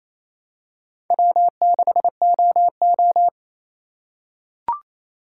Morse code, heard as a keyed tone, about 28 wpm, 700 Hz. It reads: W6OO
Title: USA Call Signs - 4 Characters - 3x Character Spacing - 28wpm